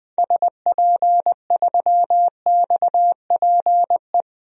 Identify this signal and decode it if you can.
Morse code, heard as a keyed tone, about 20 wpm, 700 Hz. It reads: SP3XPE
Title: DXCC-100 Call Signs - 6 Characters - Repeated 2x - 20wpm